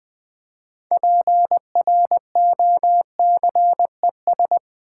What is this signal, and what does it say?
Morse code, heard as a keyed tone, about 20 wpm, 700 Hz. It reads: PROCES
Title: Build A Word - 3 Letters At A Time - 9-Letter Words - 20wpm